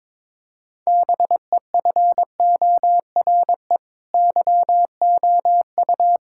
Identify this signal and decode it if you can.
Morse code, heard as a keyed tone, about 22 wpm, 700 Hz. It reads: BEFORE YOU
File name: Sentences with 9-Word Phrases - 22wpm